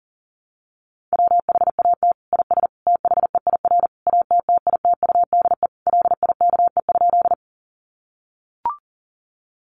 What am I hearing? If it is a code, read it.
Morse code, heard as a keyed tone, about 40 wpm, 700 Hz. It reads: WHAT IS THEIR ATTITUDE LIKE?
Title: Sentences from Top 1300 Words 40wpm